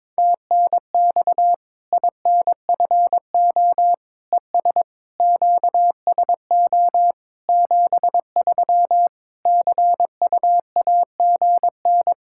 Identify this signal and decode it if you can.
Morse code, heard as a keyed tone, about 22 wpm, 700 Hz. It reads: TNX INFO ES QSO 73 CUAGN
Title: Practice QSO - Exchange 4 - Ending - 22wpm